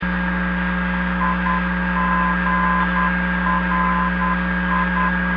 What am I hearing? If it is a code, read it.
Morse code, heard as a keyed tone, about 10 wpm, 1020 Hz, with noise. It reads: IGRI